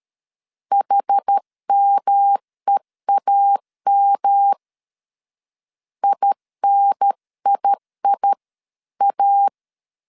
Morse code, heard as a keyed tone, about 13 wpm, 800 Hz. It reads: HMEAM INIIA
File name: CW / DET / 1 3 1500C S A 12wpm 2905